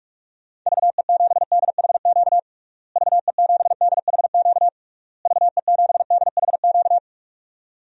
Morse code, heard as a keyed tone, about 41 wpm, 700 Hz. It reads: VE7DHQ VE7DHQ VE7DHQ
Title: DXCC-100 Call Signs - 6 Characters - Repeated 3x - 45wpm